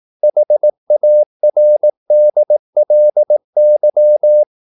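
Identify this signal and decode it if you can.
Morse code, heard as a keyed tone, about 18 wpm, 600 Hz. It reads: HARDLY
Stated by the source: Morse practice at 18 wpm, English text, from CW med Gnister.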